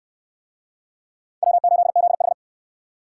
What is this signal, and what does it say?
Morse code, heard as a keyed tone, about 60 wpm, 700 Hz. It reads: W8ZH